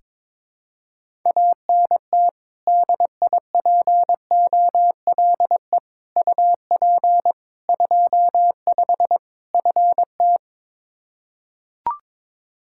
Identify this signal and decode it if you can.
Morse code, heard as a keyed tone, about 22 wpm, 700 Hz. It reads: ANT DIPOLE UP 25 FT
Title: K1USN SST QSO Practice 22wpm